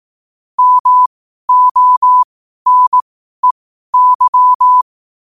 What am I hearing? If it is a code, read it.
Morse code, heard as a keyed tone, about 18 wpm, spaced as for 12 wpm, 1000 Hz. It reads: MONEY
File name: qotdfiles / 12wpm